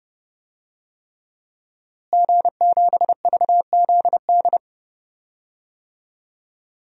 Morse code, heard as a keyed tone, about 30 wpm, 700 Hz. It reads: G7VZD
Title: DXCC-100 Call Signs - 5 Characters - Repeated 3x - 30wpm